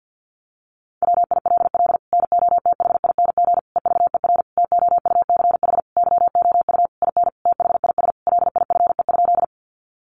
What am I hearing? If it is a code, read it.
Morse code, heard as a keyed tone, about 50 wpm, 700 Hz. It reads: WILL NOTHING EVER TOUCH YOU IN THIS LIFE?
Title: Sentences from Top 1200 Words 50wpm